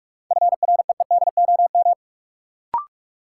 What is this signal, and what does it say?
Morse code, heard as a keyed tone, about 46 wpm, 700 Hz. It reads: FREEDOM
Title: Sentences from Top 800 Words - New Word Drill - 45wpm